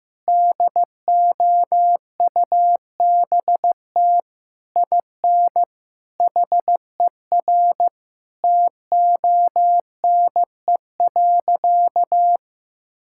Morse code, heard as a keyed tone, about 15 wpm, 700 Hz. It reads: DOUBT IN HER TONE.
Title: Sentences from Top 1600 Words 15wpm